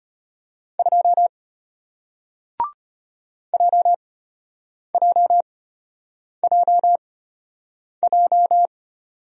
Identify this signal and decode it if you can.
Morse code, heard as a keyed tone, about 33 wpm, spaced as for 8 wpm, 700 Hz. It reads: 2 JJJJ